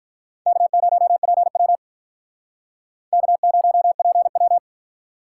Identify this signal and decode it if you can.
Morse code, heard as a keyed tone, about 46 wpm, 700 Hz. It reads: K0PW K0PW